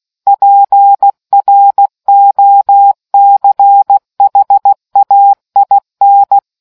Morse code, heard as a keyed tone, about 16 wpm, 800 Hz. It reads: PROCHAIN